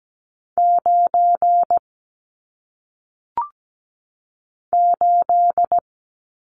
Morse code, heard as a keyed tone, about 17 wpm, 700 Hz. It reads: 9 8
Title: Numbers - 1-Digit 17wpm